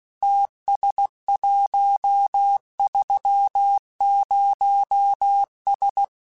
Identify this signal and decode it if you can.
Morse code, heard as a keyed tone, about 16 wpm, 800 Hz. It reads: TS130S
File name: QSOs / QSOs-mp3s / WD0BGP 1@15